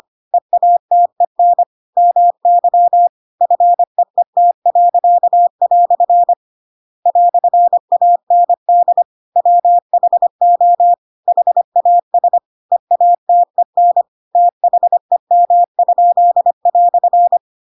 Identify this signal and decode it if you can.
Morse code, heard as a keyed tone, about 25 wpm, 700 Hz. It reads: EATEN MY FEET." "AND WHO HAS EATEN THEM?"
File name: Pinocchio / Pinocchio ch7 25 25 ik1hge